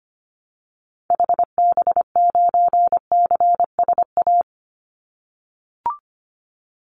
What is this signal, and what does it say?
Morse code, heard as a keyed tone, about 25 wpm, 700 Hz. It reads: HB9CSA
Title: Call Signs - 25wpm